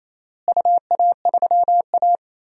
Morse code, heard as a keyed tone, about 28 wpm, 700 Hz. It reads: UA3A
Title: DXCC-100 Call Signs - 4 Characters - Repeated 2x - 28wpm